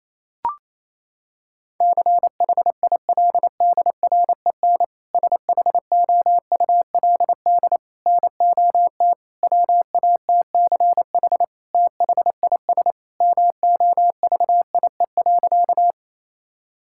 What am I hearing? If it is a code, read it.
Morse code, heard as a keyed tone, about 28 wpm, 700 Hz. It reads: CHILDREN SHOULD NOT WATCH THIS MOVIE.
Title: Sentences from Top 1000 Words - Review Part 5 - 28wpm